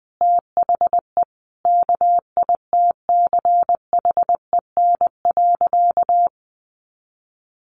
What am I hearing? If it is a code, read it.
Morse code, heard as a keyed tone, about 20 wpm, 700 Hz. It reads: THE KITCHEN.
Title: Sentences from Top 1400 Words 20wpm